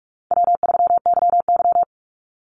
Morse code, heard as a keyed tone, about 44 wpm, 700 Hz. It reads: W3YY
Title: Easy Call Signs Repeated 3x - 45wpm